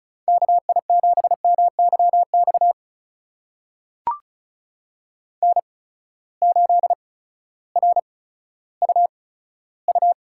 Morse code, heard as a keyed tone, about 36 wpm, spaced as for 16 wpm, 700 Hz. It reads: KI7MYX N8RUU
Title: USA Call Signs - 8x Character Spacing - 35wpm